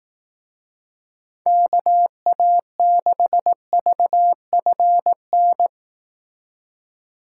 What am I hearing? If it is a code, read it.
Morse code, heard as a keyed tone, about 18 wpm, 700 Hz. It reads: KA6VFN
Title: Speed Racing - USA 2x3 Call Signs 15wpm